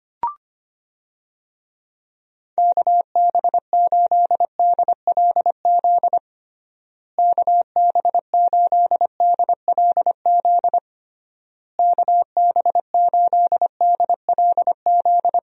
Morse code, heard as a keyed tone, about 25 wpm, 700 Hz. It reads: KB8DLZ KB8DLZ KB8DLZ
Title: USA Call Signs - 6 Characters - Repeated 3x - 25wpm